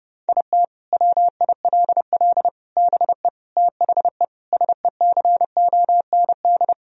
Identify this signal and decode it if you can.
Morse code, heard as a keyed tone, about 30 wpm, 700 Hz. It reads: IT WILL BE THE SECOND
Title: Sentences from Top 1000 Words - Review Part 2 - 30wpm